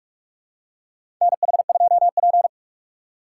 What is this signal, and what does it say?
Morse code, heard as a keyed tone, about 43 wpm, 700 Hz. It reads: NH2P
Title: USA Call Signs - 4 Characters - 2x Character Spacing - 45wpm